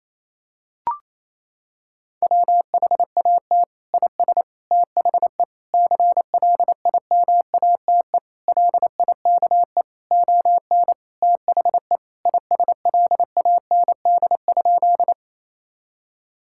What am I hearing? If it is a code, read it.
Morse code, heard as a keyed tone, about 28 wpm, 700 Hz. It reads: WHAT IS THE CLIMATE LIKE ON THE ISLAND?
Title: Sentences from Top 1600 Words 28wpm